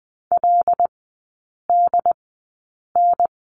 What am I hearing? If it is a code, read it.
Morse code, heard as a keyed tone, about 20 wpm, spaced as for 8 wpm, 700 Hz. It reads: LDN
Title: 212 Lesson 34e - Number 8 - Three Characters 20wpm